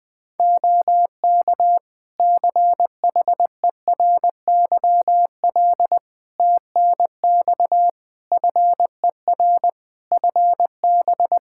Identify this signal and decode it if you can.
Morse code, heard as a keyed tone, about 21 wpm, 700 Hz. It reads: OK CHERYL TNX FER FB
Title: Practice QSO - Exchange 4 - Ending - 20wpm